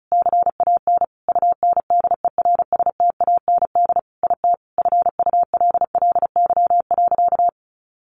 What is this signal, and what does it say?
Morse code, heard as a keyed tone, about 34 wpm, 700 Hz. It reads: CAN UNDERSTAND IT FULLY.
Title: Sentences from Top 1300 Words 35wpm